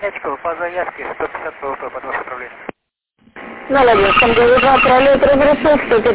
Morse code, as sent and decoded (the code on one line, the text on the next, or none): none